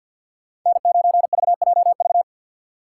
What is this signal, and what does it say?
Morse code, heard as a keyed tone, about 52 wpm, 700 Hz. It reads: N9VJV
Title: USA Call Signs - 5 Characters - 6x Character Spacing - 50wpm